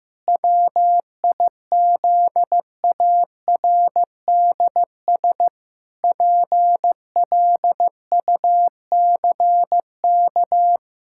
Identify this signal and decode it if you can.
Morse code, heard as a keyed tone, about 15 wpm, 700 Hz. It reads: WIZARDS PLUCK
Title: Pangram Challenge v2 - 15wpm